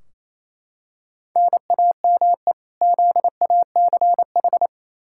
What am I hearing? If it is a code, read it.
Morse code, heard as a keyed tone, about 28 wpm, 700 Hz. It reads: NAME ZACH
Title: K1USN SST QSO Practice 28wpm